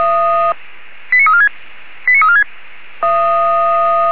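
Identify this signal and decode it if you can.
Five-tone selective call (ZVEI), heard as quick runs of five tones, tones of about 80 ms. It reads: 89346 89346
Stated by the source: ZVEI five-tone selective call received on SDR.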